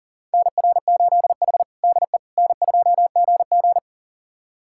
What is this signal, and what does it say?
Morse code, heard as a keyed tone, about 40 wpm, 700 Hz. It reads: NR8H DE N2GG
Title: USA Callsign DE Callsign - 4 Characters - 40wpm